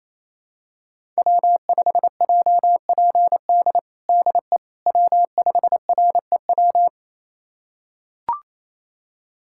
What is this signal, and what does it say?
Morse code, heard as a keyed tone, about 27 wpm, 700 Hz. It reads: W5JPD DE W5REW
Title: USA Callsign DE Callsign - 5 Characters - 28wpm